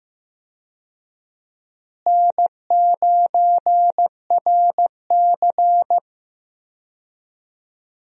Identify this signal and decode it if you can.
Morse code, heard as a keyed tone, about 16 wpm, 700 Hz. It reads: N9RC
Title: USA Call Signs - 4 Characters - Repeated 3x - 15wpm